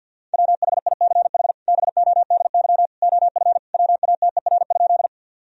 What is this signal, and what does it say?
Morse code, heard as a keyed tone, about 52 wpm, 700 Hz. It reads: WHICH BODY OF WATER?
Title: Sentences from Top 1000 Words - Review Part 2 - 50wpm